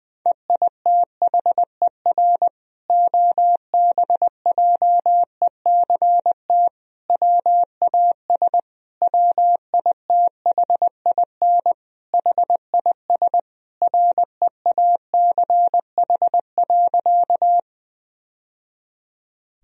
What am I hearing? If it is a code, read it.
Morse code, heard as a keyed tone, about 20 wpm, 700 Hz. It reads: EITHER OBJECT WAS WITHIN HIS REACH.
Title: Sentences from Top 1000 Words - Review Part 4 - 20wpm